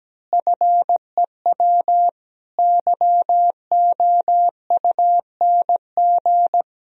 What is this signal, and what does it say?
Morse code, heard as a keyed tone, about 17 wpm, 700 Hz. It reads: FEW YOUNG
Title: Sentences from Top 1000 Words - Review Part 1 - 17wpm